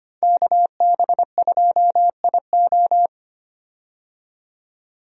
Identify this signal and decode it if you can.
Morse code, heard as a keyed tone, about 25 wpm, 700 Hz. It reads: KB2IO
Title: USA Call Signs - 5 Characters - 25wpm